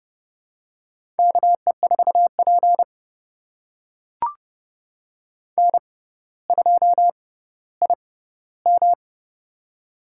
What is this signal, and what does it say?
Morse code, heard as a keyed tone, about 30 wpm, spaced as for 15 wpm, 700 Hz. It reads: KE4P N2IM
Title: USA Call Signs - 4 Characters - 6x Character Spacing - 30wpm